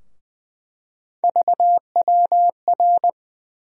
Morse code, heard as a keyed tone, about 21 wpm, 700 Hz. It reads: VWR
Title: USA Call Sign Postfixes - 3 Letters 20wpm